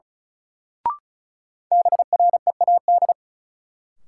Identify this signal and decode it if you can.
Morse code, heard as a keyed tone, about 36 wpm, 700 Hz. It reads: BREAD
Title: Sentences from Top 1600 Words - New Word Drill Alt - 35wpm